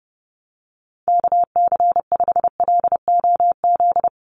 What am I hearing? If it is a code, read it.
Morse code, heard as a keyed tone, about 30 wpm, 700 Hz. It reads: KC5LOZ
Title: USA Call Signs Repeated 2x 30wpm